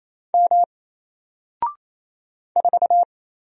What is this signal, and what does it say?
Morse code, heard as a keyed tone, about 27 wpm, 700 Hz. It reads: M 4
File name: Single-Letter-Number-28wpm